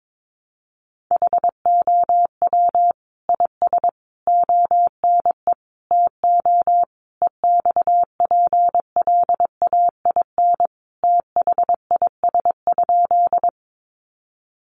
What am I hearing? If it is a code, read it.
Morse code, heard as a keyed tone, about 22 wpm, 700 Hz. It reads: HOW IS ONE TO EXPLAIN THIS?
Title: Sentences from Top 1100 Words 22wpm